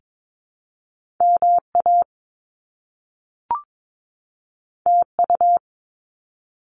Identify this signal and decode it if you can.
Morse code, heard as a keyed tone, about 22 wpm, 700 Hz. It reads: MA TU
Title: Top Two Letter Combinations in Words 22wpm